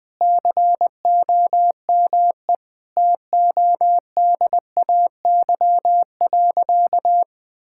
Morse code, heard as a keyed tone, about 20 wpm, 700 Hz. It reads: COME TODAY.
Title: Sentences with 5-Word Phrases - Spoken Beginning - 20wpm